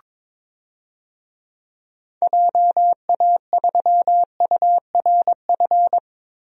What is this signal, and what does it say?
Morse code, heard as a keyed tone, about 22 wpm, 700 Hz. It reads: JA3URF
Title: DXCC-100 Call Signs - 6 Characters - Repeated 2x - 22wpm